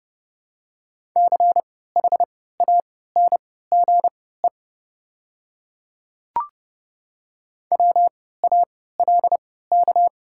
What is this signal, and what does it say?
Morse code, heard as a keyed tone, about 30 wpm, spaced as for 17 wpm, 700 Hz. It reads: CHANGE WALK
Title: 3x Character Spacing - Top 500 Words - 30wpm